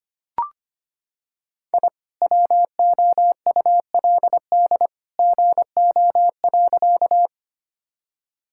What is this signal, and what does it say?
Morse code, heard as a keyed tone, about 25 wpm, 700 Hz. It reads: I WOULD GO.
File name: practice / 25wpm / Sentences with 2-Word Phrases - 25wpm